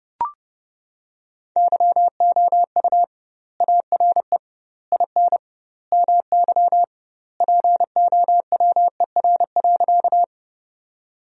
Morse code, heard as a keyed tone, about 30 wpm, 700 Hz. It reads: YOU ARE IN MY POWER.
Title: 2x Word-Spacing - Sentences from Top 500 Words - 30wpm